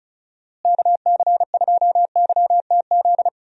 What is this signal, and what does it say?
Morse code, heard as a keyed tone, about 34 wpm, 700 Hz. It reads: KC2YTZ
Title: USA Call Signs - 6 Characters - Repeated 2x - 35wpm